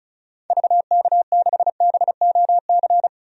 Morse code, heard as a keyed tone, about 36 wpm, 700 Hz. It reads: VK6BOC
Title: DXCC-100 Call Signs - 6 Characters - 35wpm